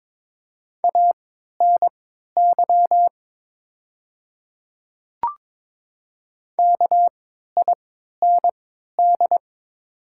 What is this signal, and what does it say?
Morse code, heard as a keyed tone, about 22 wpm, spaced as for 12 wpm, 700 Hz. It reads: ANY KIND
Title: 3x Character Spacing - Top 200 Words - 22wpm